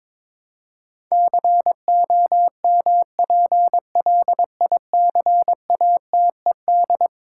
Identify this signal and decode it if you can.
Morse code, heard as a keyed tone, about 22 wpm, 700 Hz. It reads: COMPLICATED
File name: practice / 22wpm / Eleven Letter Words - 22wpm